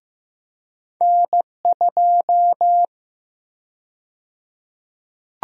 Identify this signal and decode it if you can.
Morse code, heard as a keyed tone, about 15 wpm, 700 Hz. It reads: N2
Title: USA Call Sign Prefixes - Repeated 2x - 15wpm